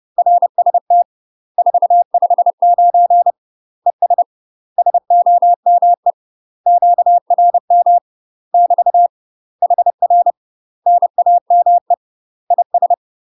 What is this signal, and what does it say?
Morse code, heard as a keyed tone, about 30 wpm, 700 Hz. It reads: RST 459 ES SOME QRM <BT> HR NAME IS
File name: UK MASTER QSOs TRAINING / 30WPM / UK MASTER QSOs 0048